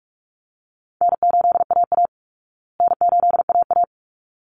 Morse code, heard as a keyed tone, about 44 wpm, 700 Hz. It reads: N8AA N8AA
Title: Call Signs Repeated 3x - 45wpm